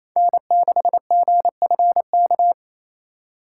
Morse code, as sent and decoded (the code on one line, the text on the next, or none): -. -.... --. ..-. -.-
N6GFK